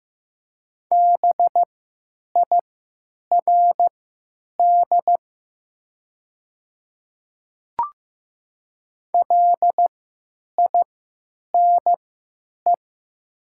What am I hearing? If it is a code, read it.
Morse code, heard as a keyed tone, about 15 wpm, spaced as for 9 wpm, 700 Hz. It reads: BIRD LINE
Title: 3x Character Spacing - Top 500 Words - 15wpm